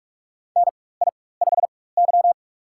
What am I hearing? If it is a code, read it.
Morse code, heard as a keyed tone, about 43 wpm, spaced as for 21 wpm, 700 Hz. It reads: NI5Y
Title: USA Call Signs - 4 Characters - 4x Character Spacing - 45wpm